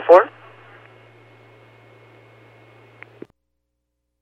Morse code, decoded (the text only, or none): none